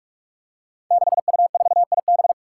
Morse code, heard as a keyed tone, about 44 wpm, 700 Hz. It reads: 6U4IB
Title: Five Challenging Letter-Number Combinations 45wpm